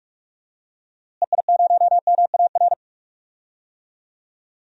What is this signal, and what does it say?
Morse code, heard as a keyed tone, about 46 wpm, 700 Hz. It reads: EI0MAR
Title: DXCC-100 Call Signs - 6 Characters - Repeated 3x - 45wpm